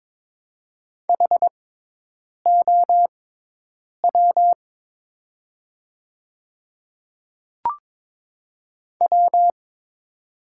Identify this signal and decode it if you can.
Morse code, heard as a keyed tone, about 23 wpm, spaced as for 8 wpm, 700 Hz. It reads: HOW W
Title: 6x Character Spacing - Top 100 Words - 22wpm